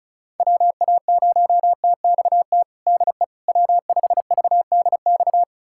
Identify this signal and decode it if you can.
Morse code, heard as a keyed tone, about 36 wpm, 700 Hz. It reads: WA0TXT DE W5VDX
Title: USA Callsign DE Callsign 35wpm